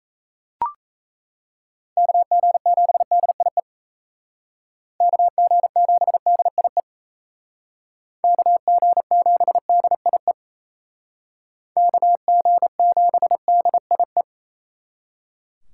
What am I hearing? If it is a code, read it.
Morse code, heard as a keyed tone, about 37 wpm, 700 Hz. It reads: KG7DIE KG7DIE KG7DIE KG7DIE